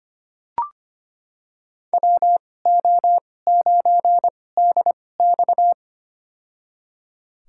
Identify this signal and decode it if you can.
Morse code, heard as a keyed tone, about 25 wpm, spaced as for 18 wpm, 700 Hz. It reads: WO9DX
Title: USA Call Signs - 5 Characters - 2x Character Spacing - 25wpm